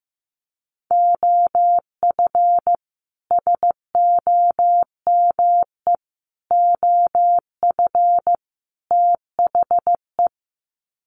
Morse code, heard as a keyed tone, about 15 wpm, 700 Hz. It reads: OF SOME OF THE
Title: Sets of 4 Words 15wpm